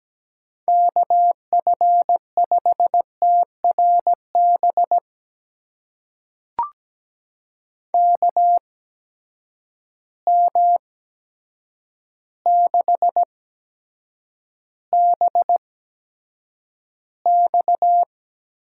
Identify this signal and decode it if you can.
Morse code, heard as a keyed tone, about 17 wpm, spaced as for 8 wpm, 700 Hz. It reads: KF5TRB KM6BX
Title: USA Call Signs - 6 Characters - 8x Character Spacing - 17wpm